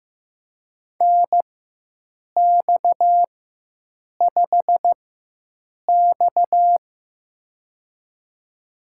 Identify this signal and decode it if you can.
Morse code, heard as a keyed tone, about 15 wpm, spaced as for 7 wpm, 700 Hz. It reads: NX5X
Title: USA Call Signs - 4 Characters - 4x Character Spacing - 15wpm